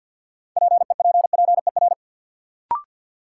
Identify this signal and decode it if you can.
Morse code, heard as a keyed tone, about 53 wpm, 700 Hz. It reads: PEPPER
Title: Sets of 5 Words - New Word Drill - 50wpm